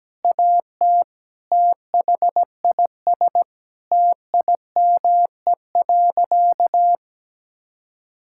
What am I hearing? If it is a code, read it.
Morse code, heard as a keyed tone, about 17 wpm, 700 Hz. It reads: AT THIS TIME.
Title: Sentences from Top 1000 Words - Review Part 5 - 17wpm